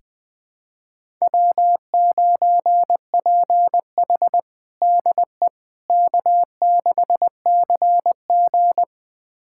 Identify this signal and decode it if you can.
Morse code, heard as a keyed tone, about 21 wpm, 700 Hz. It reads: W9PH DE K6CG
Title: USA Callsign DE Callsign - 4 Characters - 20wpm